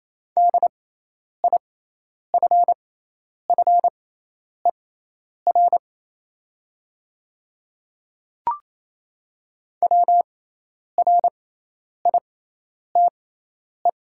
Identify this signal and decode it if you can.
Morse code, heard as a keyed tone, about 28 wpm, spaced as for 10 wpm, 700 Hz. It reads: DIFFER WRITE
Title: 6x Character Spacing - Top 200 Words - 28wpm